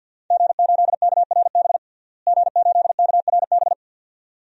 Morse code, heard as a keyed tone, about 47 wpm, 700 Hz. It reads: G8KRB G8KRB
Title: DXCC-100 Call Signs - 5 Characters - Repeated 2x - 50wpm